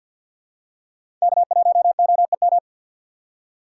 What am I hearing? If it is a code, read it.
Morse code, heard as a keyed tone, about 53 wpm, 700 Hz. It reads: K1OEM
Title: USA Call Signs - 5 Characters - 3x Character Spacing - 50wpm